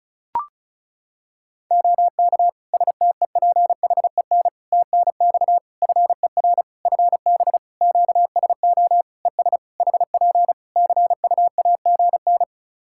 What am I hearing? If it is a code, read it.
Morse code, heard as a keyed tone, about 34 wpm, 700 Hz. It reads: OK STEPHEN TNX FER FB QSO ES HP CUAGN
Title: K1USN SST QSO Practice 35wpm